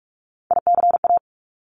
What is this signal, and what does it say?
Morse code, heard as a keyed tone, about 46 wpm, 690 Hz. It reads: ICA
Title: Top Three Letter Combinations in Words 45wpm